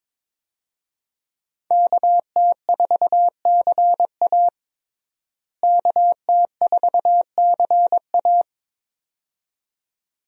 Morse code, heard as a keyed tone, about 22 wpm, 700 Hz. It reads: KT4CA KT4CA
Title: USA Call Signs - 5 Characters - Repeated 2x - 22wpm